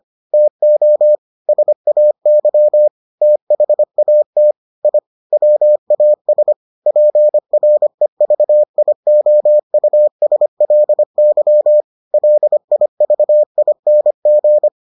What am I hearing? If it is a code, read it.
Morse code, heard as a keyed tone, about 25 wpm, 600 Hz. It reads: TO SAY THAT I WAS PREVIOUSLY LIVING